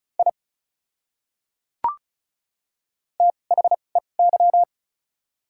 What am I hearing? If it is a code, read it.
Morse code, heard as a keyed tone, about 34 wpm, spaced as for 25 wpm, 700 Hz. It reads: I THEY
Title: 2x Character Spacing - Top 200 Words - 35wpm